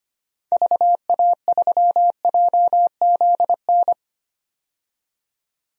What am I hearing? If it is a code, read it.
Morse code, heard as a keyed tone, about 25 wpm, 700 Hz. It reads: VA3JZN
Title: DXCC-100 Call Signs - 6 Characters - 25wpm